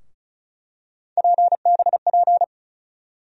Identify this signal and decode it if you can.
Morse code, heard as a keyed tone, about 36 wpm, 700 Hz. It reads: PBP
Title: USA Call Sign Postfixes - 3 Letters 35wpm